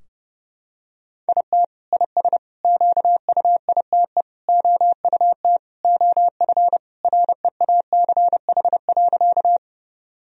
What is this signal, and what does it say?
Morse code, heard as a keyed tone, about 30 wpm, 700 Hz. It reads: IT IS QUITE OUT OF REACH.
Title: Sentences from Top 1000 Words - Review Part 2 - 30wpm